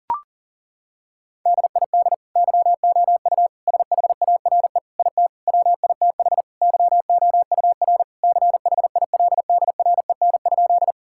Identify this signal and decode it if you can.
Morse code, heard as a keyed tone, about 40 wpm, 700 Hz. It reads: DID YOU SHARE IT WITH YOUR CHILDREN?
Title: Sentences from Top 1000 Words - Review Part 6 - 40wpm